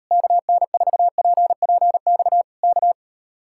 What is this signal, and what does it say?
Morse code, heard as a keyed tone, about 39 wpm, 700 Hz. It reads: KN4PPX K